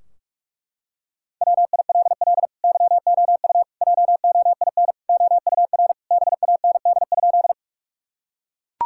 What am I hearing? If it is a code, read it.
Morse code, heard as a keyed tone, about 41 wpm, 700 Hz. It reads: WILL YOU JOIN OUR BAND?
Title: Sentences from Top 1600 Words 45wpm